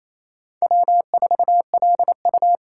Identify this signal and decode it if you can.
Morse code, heard as a keyed tone, about 35 wpm, 700 Hz. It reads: W4LU
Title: Speed Racing - USA 1x2 Call Signs 28wpm